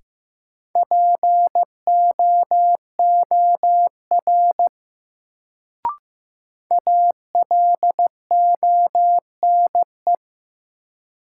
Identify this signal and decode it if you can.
Morse code, heard as a keyed tone, about 15 wpm, 700 Hz. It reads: POOR ALONE
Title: Sentences from Top 1000 Words - New Word Drill - 15wpm